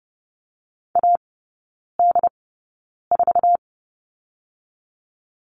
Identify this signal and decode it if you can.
Morse code, heard as a keyed tone, about 30 wpm, 700 Hz. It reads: A D 4
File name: Lesson 19e - Letter w - Three Characters 30wpm